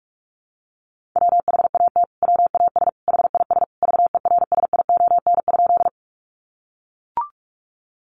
Morse code, heard as a keyed tone, about 44 wpm, 700 Hz. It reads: WHAT WAS HIS VERSION?
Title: Sentences from Top 1400 Words 45wpm